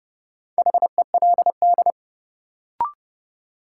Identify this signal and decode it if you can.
Morse code, heard as a keyed tone, about 32 wpm, 700 Hz. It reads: HELD
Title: Sentences from Top 1600 Words - New Word Drill Alt - 30wpm